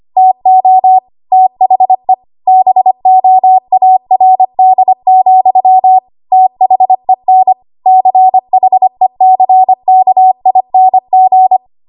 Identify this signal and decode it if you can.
Morse code, heard as a keyed tone, about 25 wpm, 750 Hz. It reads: TO THE BOARD, THEN CHECKING
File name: Archive / 25 WPM / 25WPM